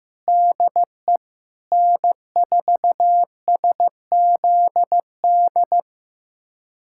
Morse code, heard as a keyed tone, about 16 wpm, 700 Hz. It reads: DE N4SZD
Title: USA Callsign DE Callsign - 5 Characters - 15wpm